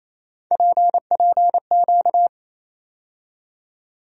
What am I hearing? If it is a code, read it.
Morse code, heard as a keyed tone, about 27 wpm, 700 Hz. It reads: PPQ
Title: USA Call Sign Postfixes - 3 Letters 28wpm